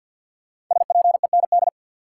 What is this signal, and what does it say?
Morse code, heard as a keyed tone, about 53 wpm, 700 Hz. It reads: SPEND
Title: Five Letter Words - 50wpm